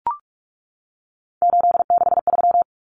Morse code, heard as a keyed tone, about 44 wpm, 700 Hz. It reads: 863
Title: Common Cut Numbers - 3-Digit 45wpm